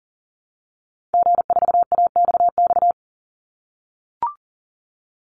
Morse code, heard as a keyed tone, about 40 wpm, 700 Hz. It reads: G4AXX
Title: Call Signs - 40wpm